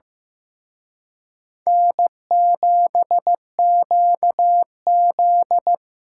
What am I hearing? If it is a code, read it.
Morse code, heard as a keyed tone, about 18 wpm, 700 Hz. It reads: N7QZ
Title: Speed Racing - USA Call Signs 15wpm